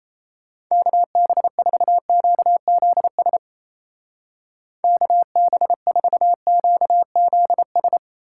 Morse code, heard as a keyed tone, about 37 wpm, 700 Hz. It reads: KB4QZS KB4QZS